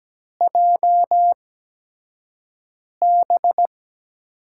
Morse code, heard as a keyed tone, about 17 wpm, spaced as for 5 wpm, 700 Hz. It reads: JB
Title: USA Call Signs - 6 Characters - 8x Character Spacing - 17wpm